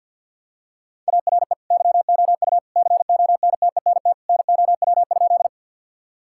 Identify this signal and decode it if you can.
Morse code, heard as a keyed tone, about 52 wpm, 700 Hz. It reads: ARE YOU CONTENT NOW?
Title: Sentences from Top 1000 Words - Review Part 6 - 50wpm